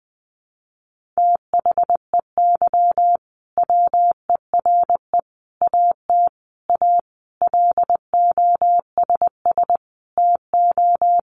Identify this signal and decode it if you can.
Morse code, heard as a keyed tone, about 20 wpm, 700 Hz. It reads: THEY WERE AT A LOSS TO
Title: Sentences from Top 1400 Words 20wpm